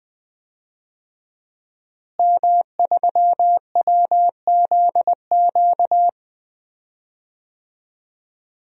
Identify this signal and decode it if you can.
Morse code, heard as a keyed tone, about 20 wpm, 700 Hz. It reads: M3WZQ
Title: DXCC-100 Call Signs - 5 Characters - Repeated 2x - 20wpm